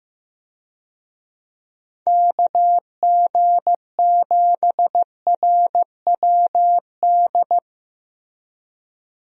Morse code, heard as a keyed tone, about 15 wpm, 700 Hz. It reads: KG7RWD
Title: USA Call Signs - 6 Characters - 15wpm